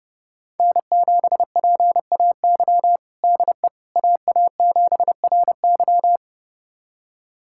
Morse code, heard as a keyed tone, about 32 wpm, 700 Hz. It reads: N7PAY DE AA7RY
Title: USA Callsign DE Callsign - 5 Characters - 30wpm